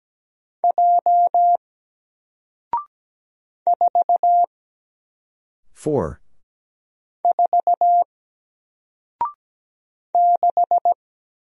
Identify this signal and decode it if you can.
Morse code, heard as a keyed tone, about 17 wpm, 700 Hz. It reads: J 4 4 6
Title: Single Letter-Number 17wpm